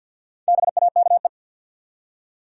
Morse code, heard as a keyed tone, about 47 wpm, 700 Hz. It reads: BAKE
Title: Four Letter Words - 50wpm